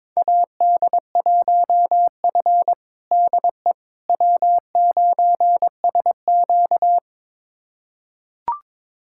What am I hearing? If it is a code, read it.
Morse code, heard as a keyed tone, about 21 wpm, 700 Hz. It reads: AD1F DE W9SQ
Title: USA Callsign DE Callsign - 4 Characters - 22wpm